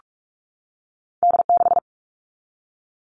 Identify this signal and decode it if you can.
Morse code, heard as a keyed tone, about 46 wpm, 700 Hz. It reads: D6
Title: All Cut Numbers - 2-Digit 45wpm